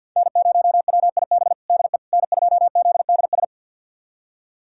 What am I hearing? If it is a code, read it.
Morse code, heard as a keyed tone, about 53 wpm, 700 Hz. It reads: N0WIB DE N2ZDS
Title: USA Callsign DE Callsign - 5 Characters - 50wpm